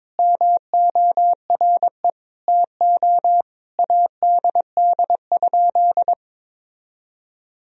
Morse code, heard as a keyed tone, about 22 wpm, 700 Hz. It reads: MORE TO ADD?
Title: Sentences from Top 1000 Words - Review Part 1 - 22wpm